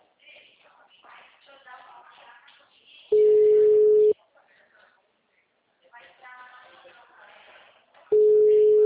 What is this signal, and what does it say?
Ringback tone (Europe, 425 Hz), with speech.